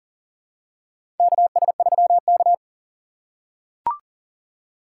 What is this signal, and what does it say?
Morse code, heard as a keyed tone, about 40 wpm, 700 Hz. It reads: KS3K
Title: USA Call Signs - 4 Characters - 6x Character Spacing - 40wpm